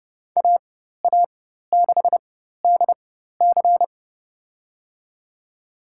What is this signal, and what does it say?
Morse code, heard as a keyed tone, about 32 wpm, spaced as for 14 wpm, 700 Hz. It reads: AA6DC
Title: USA Call Signs - 5 Characters - 4x Character Spacing - 30wpm